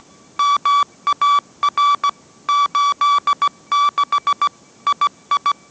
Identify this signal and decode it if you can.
Morse code, heard as a keyed tone, about 17 wpm, 1150 Hz. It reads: MAR 86 II
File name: mrx / mrx080 18wpm